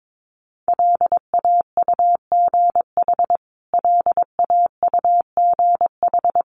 Morse code, heard as a keyed tone, about 22 wpm, 700 Hz. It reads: LAUGH LAUGH
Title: Top 500 Words Repeated 22wpm